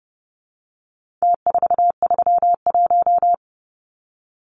Morse code, heard as a keyed tone, about 29 wpm, 700 Hz. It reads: T431
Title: Numbers - Contest Style - 4-Digit 30wpm